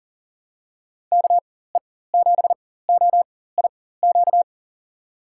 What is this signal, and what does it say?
Morse code, heard as a keyed tone, about 40 wpm, spaced as for 19 wpm, 700 Hz. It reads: KE7OIQ
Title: USA Call Signs - 4x Character Spacing - 40wpm